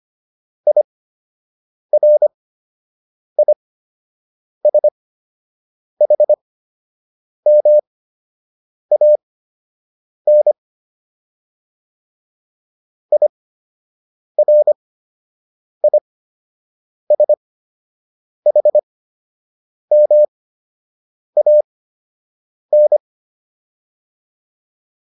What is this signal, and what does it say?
Morse code, heard as a keyed tone, about 25 wpm, spaced as for 7 wpm, 600 Hz. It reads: IRISHMAN IRISHMAN